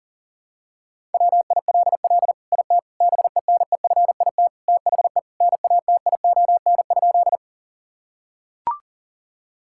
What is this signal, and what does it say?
Morse code, heard as a keyed tone, about 40 wpm, 700 Hz. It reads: WILL IT BENEFIT THE NATION?
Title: Sentences from Top 1000 Words - Review Part 6 - 40wpm